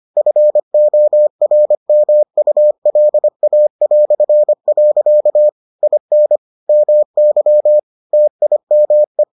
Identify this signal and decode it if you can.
Morse code, heard as a keyed tone, about 25 wpm, 600 Hz. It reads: FORMULA". IN MY TIME